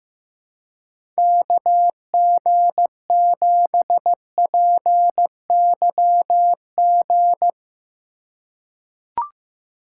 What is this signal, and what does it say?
Morse code, heard as a keyed tone, about 15 wpm, 700 Hz. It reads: KG7PYG
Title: USA Call Signs - 3x Character Spacing - 15wpm